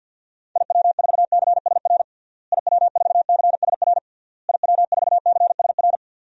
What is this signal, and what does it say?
Morse code, heard as a keyed tone, about 50 wpm, 700 Hz. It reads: IW4CSR IW4CSR IW4CSR
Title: DXCC-100 Call Signs - 6 Characters - Repeated 3x - 50wpm